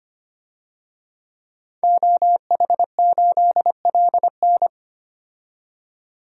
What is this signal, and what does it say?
Morse code, heard as a keyed tone, about 25 wpm, 700 Hz. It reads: OH8LN
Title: DXCC-100 Call Signs - 5 Characters - 25wpm